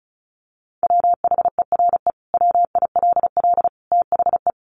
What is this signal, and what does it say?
Morse code, heard as a keyed tone, about 34 wpm, 700 Hz. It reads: WHERE WILL THE
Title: Sentences from Top 1500 Words 35wpm